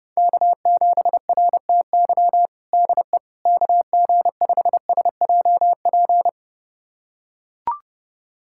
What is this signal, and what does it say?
Morse code, heard as a keyed tone, about 30 wpm, 700 Hz. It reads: K7RTY DE KG5SJP
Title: USA Callsign DE Callsign 30wpm